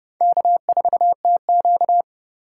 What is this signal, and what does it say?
Morse code, heard as a keyed tone, about 29 wpm, 700 Hz. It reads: K4TQ